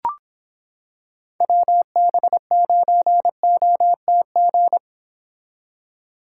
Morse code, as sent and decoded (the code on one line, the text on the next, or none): .-- -... ----. --- - --.
WB9OTG